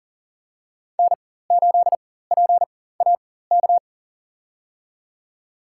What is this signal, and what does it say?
Morse code, heard as a keyed tone, about 40 wpm, spaced as for 19 wpm, 700 Hz. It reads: N8PAK
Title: USA Call Signs - 5 Characters - 4x Character Spacing - 40wpm